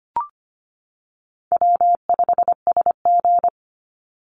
Morse code, heard as a keyed tone, about 25 wpm, 700 Hz. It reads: W5SG
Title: Call Signs - 25wpm